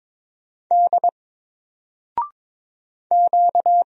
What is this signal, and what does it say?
Morse code, heard as a keyed tone, about 22 wpm, 700 Hz. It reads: D Q
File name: Single-Letter-Number-22wpm